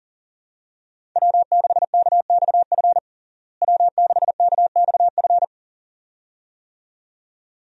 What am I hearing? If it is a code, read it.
Morse code, heard as a keyed tone, about 40 wpm, 700 Hz. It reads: W6KXF W6KXF
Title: USA Call Signs - 5 Characters - Repeated 2x - 40wpm